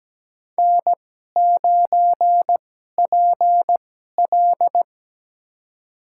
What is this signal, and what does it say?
Morse code, heard as a keyed tone, about 17 wpm, spaced as for 12 wpm, 700 Hz. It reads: N9PL
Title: USA Call Signs - 4 Characters - 2x Character Spacing - 17wpm